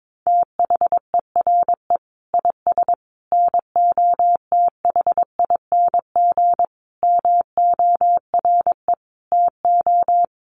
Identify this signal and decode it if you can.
Morse code, heard as a keyed tone, about 22 wpm, 700 Hz. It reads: THERE IS NOTHING MORE TO
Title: Sentences from Top 500 Words 22wpm